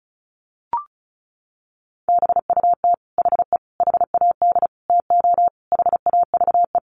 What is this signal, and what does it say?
Morse code, heard as a keyed tone, about 34 wpm, 700 Hz. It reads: BUT HE HAD TO HAVE
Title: Sentences from Top 100 Words 35wpm